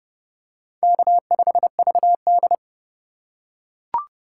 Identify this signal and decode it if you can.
Morse code, heard as a keyed tone, about 32 wpm, 700 Hz. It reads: K5VD
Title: USA Call Signs - 4 Characters - 2x Character Spacing - 30wpm